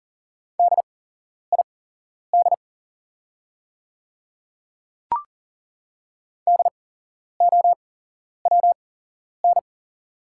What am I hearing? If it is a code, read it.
Morse code, heard as a keyed tone, about 40 wpm, spaced as for 11 wpm, 690 Hz. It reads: DID DOWN
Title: 8x Character Spacing - Top 200 Words - 40wpm